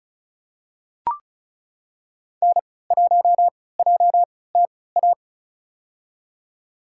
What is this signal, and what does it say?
Morse code, heard as a keyed tone, about 34 wpm, spaced as for 20 wpm, 700 Hz. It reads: N1JTA
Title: USA Call Signs - 5 Characters - 3x Character Spacing - 35wpm